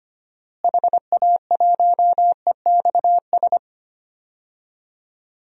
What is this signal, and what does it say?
Morse code, heard as a keyed tone, about 25 wpm, 700 Hz. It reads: HA1EXS
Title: DXCC-100 Call Signs - 6 Characters - Repeated 3x - 25wpm